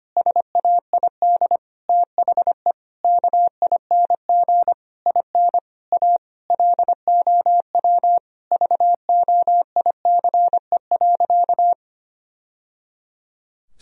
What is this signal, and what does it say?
Morse code, heard as a keyed tone, about 25 wpm, 700 Hz. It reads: SAID THE KING IN A LOW VOICE.
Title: Sentences from Top 1000 Words - Review Part 1 - 25wpm